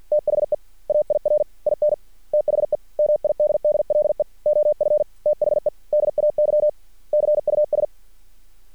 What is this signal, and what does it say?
Morse code, heard as a keyed tone, about 50 wpm, 600 Hz. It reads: THE AIR IN THE MIDDLE OF THE DAY CUS
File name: archived / CRSnet